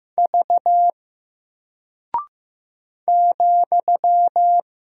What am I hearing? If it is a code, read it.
Morse code, heard as a keyed tone, about 15 wpm, 700 Hz. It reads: V ,
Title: Single Character (Letter:Number:Punctuation:Prosign) 15wpm